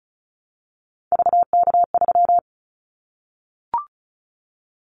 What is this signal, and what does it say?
Morse code, heard as a keyed tone, about 36 wpm, 700 Hz. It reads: VK3
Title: Callsign Prefixes for Top 100 DXCC - 35wpm